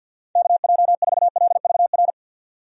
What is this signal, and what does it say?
Morse code, heard as a keyed tone, about 52 wpm, 700 Hz. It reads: KJ4LVR
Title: USA Call Signs - 6 Characters - 50wpm